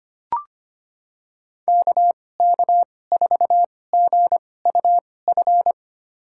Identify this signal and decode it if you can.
Morse code, heard as a keyed tone, about 25 wpm, spaced as for 18 wpm, 700 Hz. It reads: KK4GUF
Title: USA Call Signs - 6 Characters - 2x Character Spacing - 25wpm